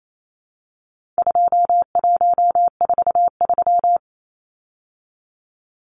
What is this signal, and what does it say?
Morse code, heard as a keyed tone, about 28 wpm, 700 Hz. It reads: 2143
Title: Numbers - 4-Digit 28wpm